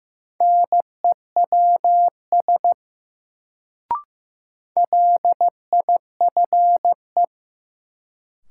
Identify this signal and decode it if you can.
Morse code, heard as a keyed tone, about 15 wpm, 700 Hz. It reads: NEWS LIFE
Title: Sets of 5 Words - New Word Drill - 15wpm